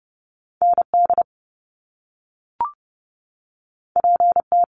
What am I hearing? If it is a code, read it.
Morse code, heard as a keyed tone, about 30 wpm, 700 Hz. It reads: ND PT
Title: Top Two Letter Combinations in Words 30wpm